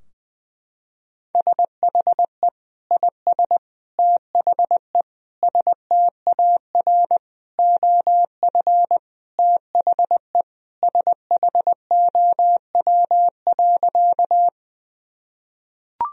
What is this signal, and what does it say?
Morse code, heard as a keyed tone, about 20 wpm, 700 Hz. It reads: SHE IS THE STAR OF THE SHOW.
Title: Sentences from Top 1000 Words - Review Part 2 - 20wpm